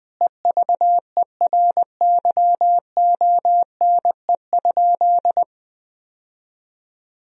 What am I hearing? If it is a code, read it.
Morse code, heard as a keyed tone, about 19 wpm, 700 Hz. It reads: EVERYONE?
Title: Sentences from Top 1000 Words - Review Part 4 - 20wpm